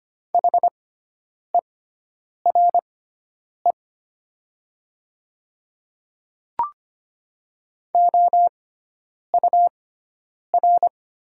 Morse code, heard as a keyed tone, about 25 wpm, spaced as for 9 wpm, 700 Hz. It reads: HERE OUR
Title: 6x Character Spacing - Top 200 Words - 25wpm